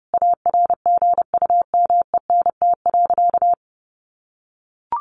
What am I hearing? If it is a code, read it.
Morse code, heard as a keyed tone, about 30 wpm, 700 Hz. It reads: ARGUMENT.
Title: Sentences from Top 1400 Words 30wpm